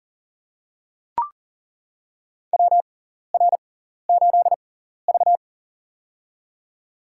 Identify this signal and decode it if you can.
Morse code, heard as a keyed tone, about 40 wpm, spaced as for 14 wpm, 700 Hz. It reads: WR8V
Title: USA Call Signs - 4 Characters - 6x Character Spacing - 40wpm